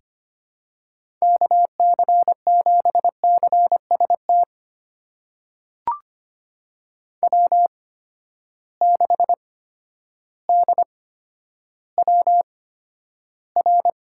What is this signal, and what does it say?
Morse code, heard as a keyed tone, about 25 wpm, spaced as for 11 wpm, 700 Hz. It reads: KC7CST W6DWR
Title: USA Call Signs - 8x Character Spacing - 25wpm